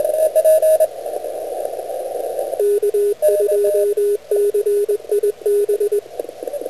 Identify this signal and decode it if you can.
Morse code, heard as a keyed tone, about 20 wpm, 410 Hz, with radio noise. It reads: K2CIB